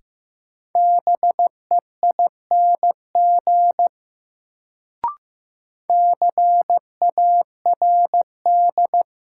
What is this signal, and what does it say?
Morse code, heard as a keyed tone, about 15 wpm, 700 Hz. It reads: BEING CARD
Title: Sentences from Top 600 Words - New Word Drill - 15wpm